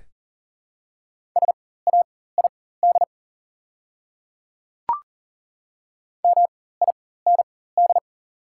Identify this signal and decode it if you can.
Morse code, heard as a keyed tone, about 37 wpm, spaced as for 18 wpm, 690 Hz. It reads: SAID MIND
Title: 4x Character Spacing - Top 500 Words - 40wpm